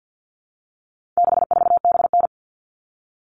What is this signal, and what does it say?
Morse code, heard as a keyed tone, about 50 wpm, 700 Hz. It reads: 64BN
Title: All Cut Numbers - 4-Digit 50wpm